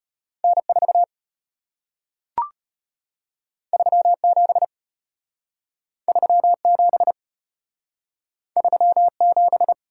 Morse code, heard as a keyed tone, about 33 wpm, 700 Hz. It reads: N 4 3 7 3 7 3 7